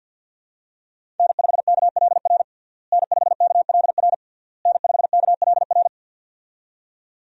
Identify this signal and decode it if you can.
Morse code, heard as a keyed tone, about 51 wpm, 700 Hz. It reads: N5KLR N5KLR N5KLR
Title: USA Call Signs - 5 Characters - Repeated 3x - 50wpm